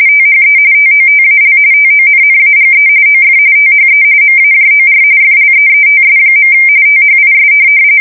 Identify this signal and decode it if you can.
RTTY, about 45.45 baud. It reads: THE TOTAL TONNAGE OF THE MAJOR FLEETS. THIS
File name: fsk170 / USS-New-Jersey